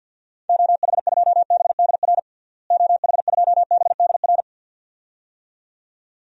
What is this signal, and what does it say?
Morse code, heard as a keyed tone, about 52 wpm, 700 Hz. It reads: OH2BDR OH2BDR
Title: DXCC-100 Call Signs - 6 Characters - Repeated 2x - 50wpm